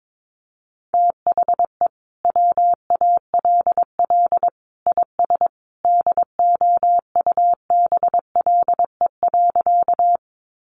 Morse code, heard as a keyed tone, about 22 wpm, 700 Hz. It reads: THE WALL IS DOUBLE.
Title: Sentences from Top 1400 Words 22wpm